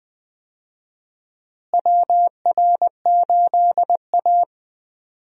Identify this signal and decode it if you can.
Morse code, heard as a keyed tone, about 20 wpm, 700 Hz. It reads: WR8A
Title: USA Call Signs - 4 Characters - Repeated 2x - 20wpm